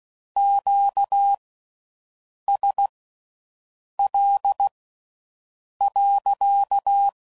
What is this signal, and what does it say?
Morse code, heard as a keyed tone, about 16 wpm, spaced as for 6 wpm, 800 Hz. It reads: QSL.